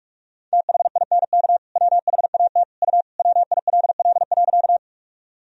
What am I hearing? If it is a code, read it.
Morse code, heard as a keyed tone, about 41 wpm, 700 Hz. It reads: THINK WHAT U WILL.
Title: Sentences from Top 200 Words and Abbreviations 45wpm